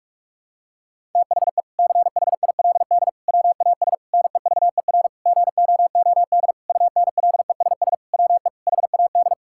Morse code, heard as a keyed tone, about 46 wpm, 700 Hz. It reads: THE CHILD WAS NEVER GOOD UNLESS WE HAD
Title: Sentences from Top 1000 Words - Review Part 6 - 45wpm